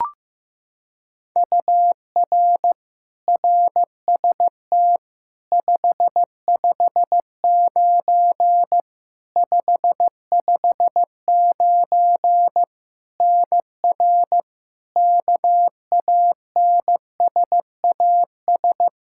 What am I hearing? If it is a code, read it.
Morse code, heard as a keyed tone, about 15 wpm, 700 Hz. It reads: UR RST 559 559 NR KANSAS
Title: K1USN SST QSO Practice 15wpm